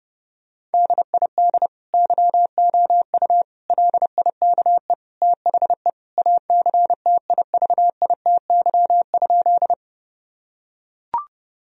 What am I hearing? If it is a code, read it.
Morse code, heard as a keyed tone, about 32 wpm, 700 Hz. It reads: DID YOU LIKE THE ACTIVITY?
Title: Sentences from Top 1000 Words - Review Part 4 - 30wpm